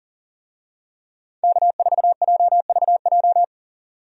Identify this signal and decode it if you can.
Morse code, heard as a keyed tone, about 40 wpm, 700 Hz. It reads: K4JVJ
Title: USA Call Signs - 5 Characters - 40wpm